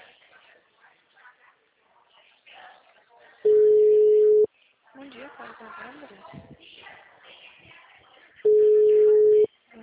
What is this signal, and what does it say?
Ringback tone (Europe, 425 Hz), with speech.
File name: out- -2009- - - .50